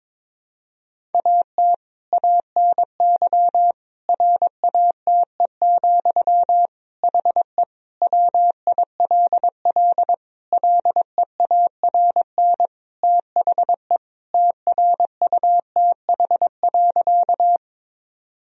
Morse code, heard as a keyed tone, about 22 wpm, 700 Hz. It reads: AT ANY RATE, HE WILL LEARN THE TRUTH.
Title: Sentences from Top 1000 Words - Review Part 4 - 22wpm